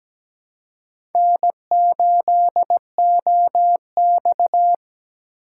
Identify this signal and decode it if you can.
Morse code, heard as a keyed tone, about 17 wpm, 700 Hz. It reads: N8OX
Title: USA Call Signs - 4 Characters - 4x Character Spacing - 17wpm